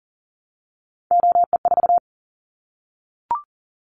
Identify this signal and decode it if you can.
Morse code, heard as a keyed tone, about 43 wpm, 700 Hz. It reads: OE4
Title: Callsign Prefixes for Top 50 DXCC - 40wpm